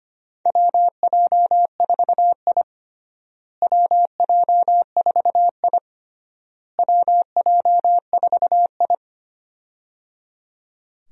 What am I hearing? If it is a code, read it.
Morse code, heard as a keyed tone, about 25 wpm, 700 Hz. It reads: WJ4I WJ4I WJ4I
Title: USA Call Signs - 4 Characters - Repeated 3x - 25wpm